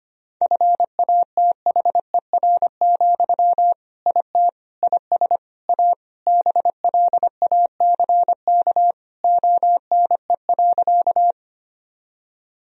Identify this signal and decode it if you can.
Morse code, heard as a keyed tone, about 25 wpm, 700 Hz. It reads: FATHER, IT IS A BLACK ONE.
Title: Sentences from Top 1000 Words - Review Part 2 - 25wpm